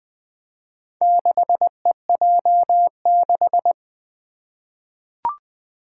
Morse code, heard as a keyed tone, about 20 wpm, 700 Hz. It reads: 6EJ6
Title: Four Challenging Letter-Number Combinations 20wpm